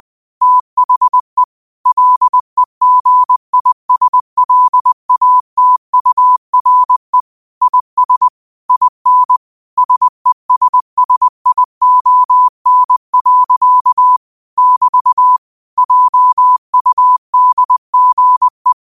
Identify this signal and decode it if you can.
Morse code, heard as a keyed tone, about 20 wpm, 1000 Hz. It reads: THE LEGISLATURE IS IN SESSION. <BT> JUDGE
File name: qotdfiles / 20wpm